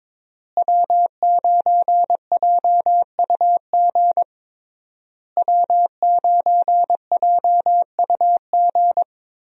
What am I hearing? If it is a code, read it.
Morse code, heard as a keyed tone, about 22 wpm, 700 Hz. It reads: W9JUG W9JUG
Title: USA Call Signs - 5 Characters - Repeated 3x - 22wpm